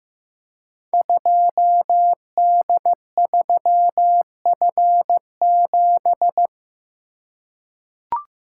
Morse code, heard as a keyed tone, about 15 wpm, 700 Hz. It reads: 2D3F7
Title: Code Groups - 5 Letter:Numbers - 15wpm